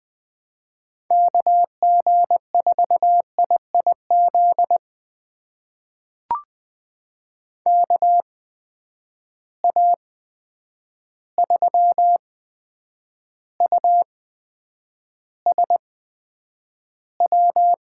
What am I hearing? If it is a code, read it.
Morse code, heard as a keyed tone, about 20 wpm, spaced as for 9 wpm, 700 Hz. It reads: KG4IIZ KA3USW
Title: USA Call Signs - 6 Characters - 8x Character Spacing - 20wpm